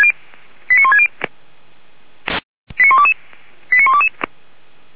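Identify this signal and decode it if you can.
Five-tone selective call (ZVEI), heard as quick runs of five tones, tones of about 80 ms. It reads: 89160 89122 89122
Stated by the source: ZVEI five-tone selective call 89185 received by SDR.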